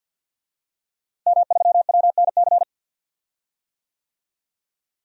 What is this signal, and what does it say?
Morse code, heard as a keyed tone, about 50 wpm, 700 Hz. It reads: M3WNC
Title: DXCC-100 Call Signs - 5 Characters - Repeated 2x - 50wpm